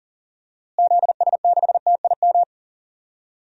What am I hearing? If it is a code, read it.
Morse code, heard as a keyed tone, about 37 wpm, 700 Hz. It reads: ZS6TIM
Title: DXCC-100 Call Signs - 6 Characters - Repeated 2x - 40wpm